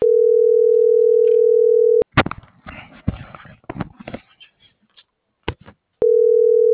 Ringback tone (North America), 440 + 480 Hz.